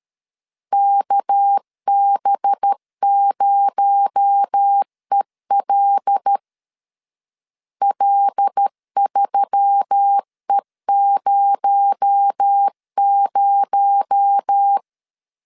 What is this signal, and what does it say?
Morse code, heard as a keyed tone, about 13 wpm, 800 Hz. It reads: KB0EL L3E00
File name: LEZIONI / 6 PKX0 / 6 PKX0 12wpm 2231